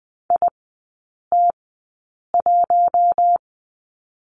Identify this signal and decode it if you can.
Morse code, heard as a keyed tone, about 20 wpm, spaced as for 8 wpm, 700 Hz. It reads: IT1
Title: 041 Lesson 8e - Number 1 - Three Characters 20wpm